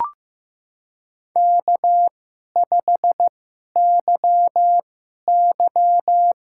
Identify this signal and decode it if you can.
Morse code, heard as a keyed tone, about 15 wpm, spaced as for 11 wpm, 700 Hz. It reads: K5YY
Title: USA Call Signs - 4 Characters - 2x Character Spacing - 15wpm